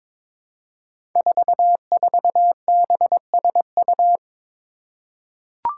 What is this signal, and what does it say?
Morse code, heard as a keyed tone, about 22 wpm, 700 Hz. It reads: 44BSU
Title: Five Challenging Letter-Number Combinations 22wpm